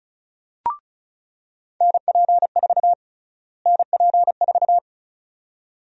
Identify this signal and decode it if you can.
Morse code, heard as a keyed tone, about 34 wpm, 700 Hz. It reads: NP4 NP4
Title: USA Call Sign Prefixes - Repeated 2x - 35wpm